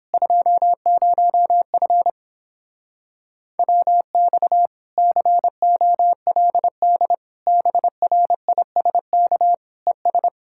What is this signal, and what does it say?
Morse code, heard as a keyed tone, about 31 wpm, 700 Hz. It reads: 20F WX COLD BRISK ES